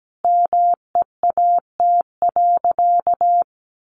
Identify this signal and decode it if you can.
Morse code, heard as a keyed tone, about 17 wpm, 700 Hz. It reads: MEAT.
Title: Sentences from Top 1500 Words 17wpm